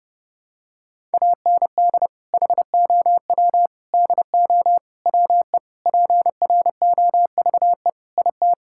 Morse code, heard as a keyed tone, about 32 wpm, 700 Hz. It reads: AND HOW DO WE PROVE IT
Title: Sentences from Top 1600 Words 30wpm